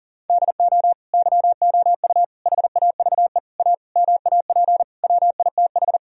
Morse code, heard as a keyed tone, about 37 wpm, 700 Hz. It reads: DO YOU HAVE A MAP WITH
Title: Sentences from Top 1000 Words - Review Part 2 - 40wpm